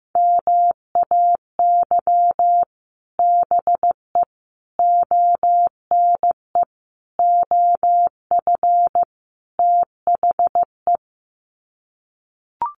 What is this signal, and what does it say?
Morse code, heard as a keyed tone, about 15 wpm, 700 Hz. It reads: MAY BE ONE OF THE
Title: Sets of 5 Words Encore2 15wpm